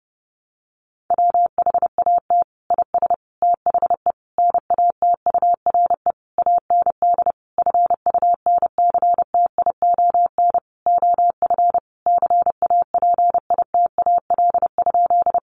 Morse code, heard as a keyed tone, about 30 wpm, 700 Hz. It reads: WHAT IS THE NATURE AND FUNCTION OF CAPITAL?
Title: Sentences from Top 1100 Words 30wpm